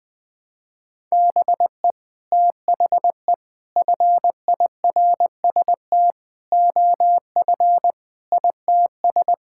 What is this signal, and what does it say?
Morse code, heard as a keyed tone, about 20 wpm, 700 Hz. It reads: BE THE FIRST OF ITS
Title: Sets of 8 Words - 20wpm